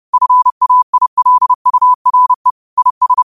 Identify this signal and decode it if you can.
Morse code, heard as a keyed tone, about 30 wpm, 1000 Hz. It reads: FAILURE IS
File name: qotdfiles / 30wpm